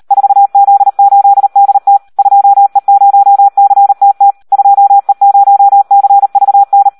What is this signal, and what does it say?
Morse code, heard as a keyed tone, about 38 wpm, 800 Hz. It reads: 4Z8DT 2E0CTT 2E0CVN
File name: Audio practica CW / SD 1 38